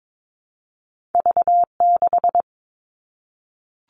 Morse code, heard as a keyed tone, about 22 wpm, 700 Hz. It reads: V6
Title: Two Challenging Letter-Number Combinations 22wpm